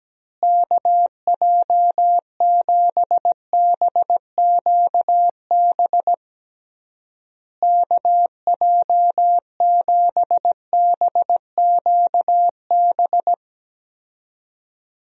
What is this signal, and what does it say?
Morse code, heard as a keyed tone, about 17 wpm, 700 Hz. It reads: KJ7BQB KJ7BQB
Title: USA Call Signs - 6 Characters - Repeated 3x - 17wpm